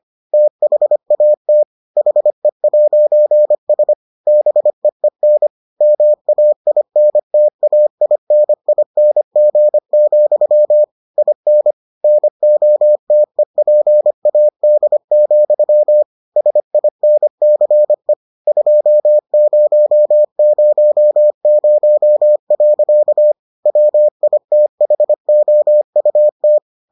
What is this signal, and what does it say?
Morse code, heard as a keyed tone, about 25 wpm, 600 Hz. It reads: THAT HE'S BEEN MAINTAINING, IN NOTEPAD, SINCE 2000. WITHOUT